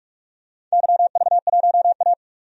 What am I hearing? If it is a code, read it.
Morse code, heard as a keyed tone, about 43 wpm, 700 Hz. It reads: YV1A
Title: DXCC-100 Call Signs - 4 Characters - 45wpm